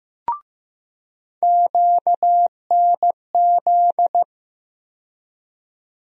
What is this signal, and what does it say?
Morse code, heard as a keyed tone, about 15 wpm, 700 Hz. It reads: QNZ
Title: USA Call Sign Postfixes - 3 Letters 15wpm